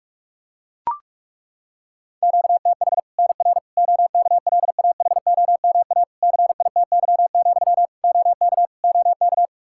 Morse code, heard as a keyed tone, about 41 wpm, 700 Hz. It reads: QTH NR OKLAHOMA CITY, OK OK
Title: Practice QSO - Exchange 1 - Beginning - 45wpm